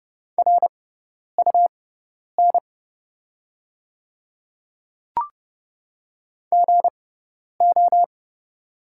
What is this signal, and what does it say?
Morse code, heard as a keyed tone, about 30 wpm, spaced as for 10 wpm, 700 Hz. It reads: RUN GO
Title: 6x Character Spacing - Top 300 Words - 30wpm